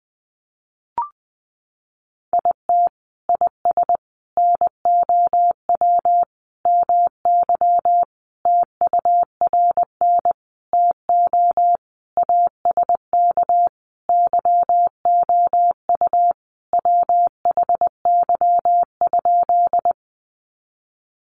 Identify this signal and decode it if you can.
Morse code, heard as a keyed tone, about 20 wpm, 700 Hz. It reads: IT IS NOW MY TURN TO ASK YOU WHY?
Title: Sentences from Top 200 Words 20wpm